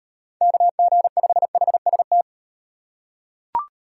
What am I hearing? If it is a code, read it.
Morse code, heard as a keyed tone, about 37 wpm, 700 Hz. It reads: KG5HST